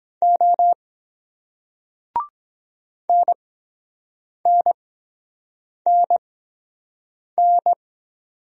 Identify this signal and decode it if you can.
Morse code, heard as a keyed tone, about 23 wpm, spaced as for 7 wpm, 700 Hz. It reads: O NNNN